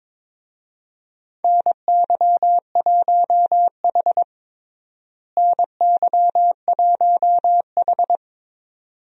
Morse code, heard as a keyed tone, about 22 wpm, 700 Hz. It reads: NY1H NY1H
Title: USA Call Signs - 4 Characters - Repeated 2x - 22wpm